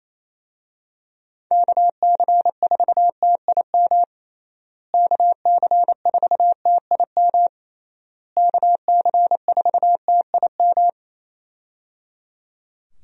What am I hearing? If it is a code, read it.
Morse code, heard as a keyed tone, about 28 wpm, 700 Hz. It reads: KC4TIM KC4TIM KC4TIM
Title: USA Call Signs - 6 Characters - Repeated 3x - 28wpm